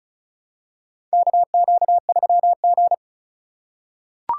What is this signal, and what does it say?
Morse code, heard as a keyed tone, about 36 wpm, 700 Hz. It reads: KQ3G
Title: USA Call Signs - 4 Characters - 2x Character Spacing - 35wpm